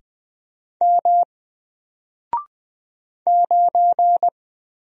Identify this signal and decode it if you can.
Morse code, heard as a keyed tone, about 20 wpm, 700 Hz. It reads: M 9
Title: Single Letter-Number Alt 20wpm